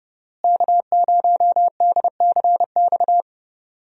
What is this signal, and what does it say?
Morse code, heard as a keyed tone, about 30 wpm, 700 Hz. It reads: K0DCX
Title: USA Call Signs - 5 Characters - Repeated 2x - 30wpm